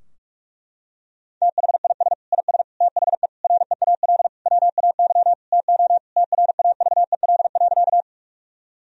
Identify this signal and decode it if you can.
Morse code, heard as a keyed tone, about 46 wpm, 700 Hz. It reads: THIS IS THE REAL WAY TO TRAVEL.
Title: Sentences from Top 1000 Words - Review Part 2 - 45wpm